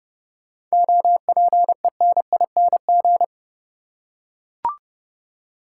Morse code, heard as a keyed tone, about 31 wpm, 700 Hz. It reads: OPENING
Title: Seven Letter Words - 30wpm